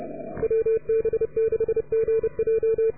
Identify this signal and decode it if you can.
Morse code, heard as a keyed tone, about 29 wpm, 440 Hz. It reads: WB6GJ